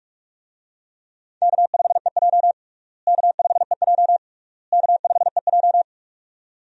Morse code, heard as a keyed tone, about 41 wpm, 700 Hz. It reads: K5EJ K5EJ K5EJ
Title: USA Call Signs - 4 Characters - Repeated 3x - 45wpm